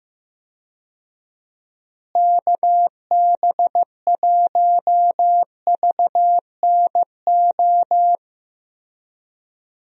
Morse code, heard as a keyed tone, about 15 wpm, 700 Hz. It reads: KB1VNO
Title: USA Call Signs - 6 Characters - Repeated 3x - 15wpm